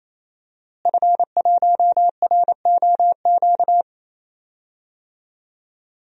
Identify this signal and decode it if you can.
Morse code, heard as a keyed tone, about 28 wpm, 700 Hz. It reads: F1ROQ
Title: DXCC-100 Call Signs - 5 Characters - Repeated 3x - 28wpm